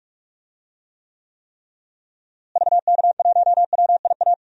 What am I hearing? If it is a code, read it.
Morse code, heard as a keyed tone, about 46 wpm, 700 Hz. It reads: VK1WIA
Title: DXCC-100 Call Signs - 6 Characters - Repeated 3x - 45wpm